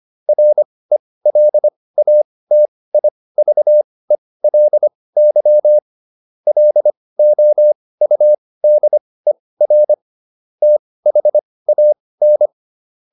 Morse code, heard as a keyed tone, about 25 wpm, spaced as for 18 wpm, 600 Hz. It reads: RELATIVELY LOUDER THAN